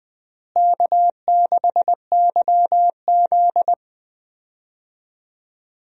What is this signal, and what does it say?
Morse code, heard as a keyed tone, about 20 wpm, 700 Hz. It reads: K6YZ
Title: USA Call Signs - 4 Characters - Repeated 3x - 20wpm